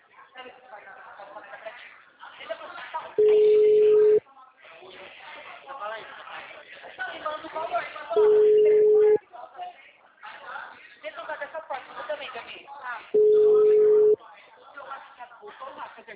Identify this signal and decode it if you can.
Ringback tone (Europe, 425 Hz), with speech.